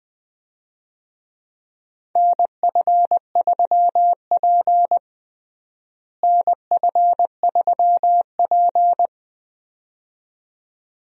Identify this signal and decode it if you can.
Morse code, heard as a keyed tone, about 21 wpm, 700 Hz. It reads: NF3P NF3P
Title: USA Call Signs - 4 Characters - Repeated 2x - 20wpm